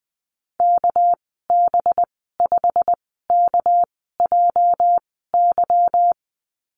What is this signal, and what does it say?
Morse code, heard as a keyed tone, about 20 wpm, spaced as for 15 wpm, 700 Hz. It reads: KB5KJY
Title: 231 Lesson 37g - Letter x - Call Signs 20wpm